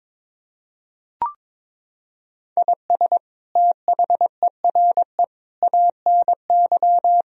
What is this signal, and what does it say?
Morse code, heard as a keyed tone, about 22 wpm, 700 Hz. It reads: IS THERE ANY
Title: Sentences from Top 1000 Words - Review Part 4 - 22wpm